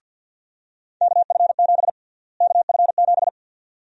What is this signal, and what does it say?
Morse code, heard as a keyed tone, about 52 wpm, 700 Hz. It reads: KF7 KF7
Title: USA Call Sign Prefixes - Repeated 2x - 50wpm